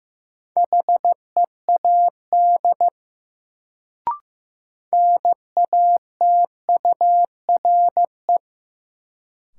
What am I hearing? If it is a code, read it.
Morse code, heard as a keyed tone, about 15 wpm, 700 Hz. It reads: HEAD NATURE
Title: Sets of 3 Words - New Word Drill - 15wpm